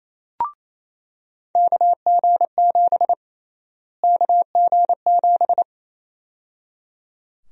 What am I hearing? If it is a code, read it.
Morse code, heard as a keyed tone, about 28 wpm, 700 Hz. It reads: KG7 KG7
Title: USA Call Sign Prefixes - Repeated 2x - 28wpm